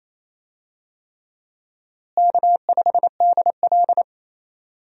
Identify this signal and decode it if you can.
Morse code, heard as a keyed tone, about 28 wpm, 700 Hz. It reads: K5DL
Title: USA Call Signs - 4 Characters - Repeated 3x - 28wpm